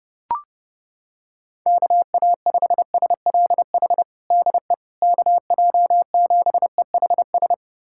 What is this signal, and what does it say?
Morse code, heard as a keyed tone, about 30 wpm, 700 Hz. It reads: KA5SLH DE KJ7EHS
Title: USA Callsign DE Callsign 30wpm